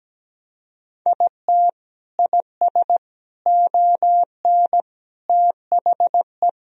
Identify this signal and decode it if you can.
Morse code, heard as a keyed tone, about 17 wpm, 700 Hz. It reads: IT IS ON THE
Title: Sentences from Top 1000 Words - Review Part 2 - 17wpm